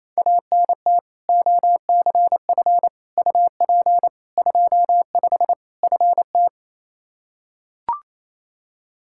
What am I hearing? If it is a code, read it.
Morse code, heard as a keyed tone, about 28 wpm, 700 Hz. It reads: ANT OCF UP 25 FT
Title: Practice QSO - Exchange 2 - Info - 28wpm